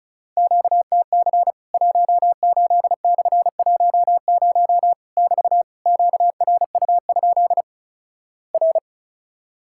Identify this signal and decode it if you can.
Morse code, heard as a keyed tone, about 36 wpm, 700 Hz. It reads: QTC 18/10 <BT> QRU? R
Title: Worked All Europe DX Contest - QTC Exchange - 35wpm